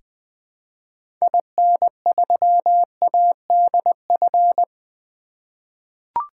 Morse code, heard as a keyed tone, about 20 wpm, 700 Hz. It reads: IN3ADF
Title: 2023 SOTA Activator Call Signs - 20wpm